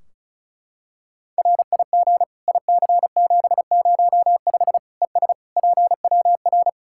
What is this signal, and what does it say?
Morse code, heard as a keyed tone, about 36 wpm, 700 Hz. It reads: RIG IC705 ES PWR
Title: Practice QSO - Exchange 2 - Info - 35wpm